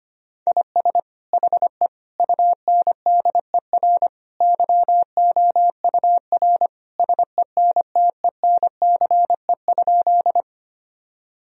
Morse code, heard as a keyed tone, about 25 wpm, 700 Hz. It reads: IS HE UNDER YOUR SENTENCE?
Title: Sentences from Top 1000 Words - Review Part 1 - 25wpm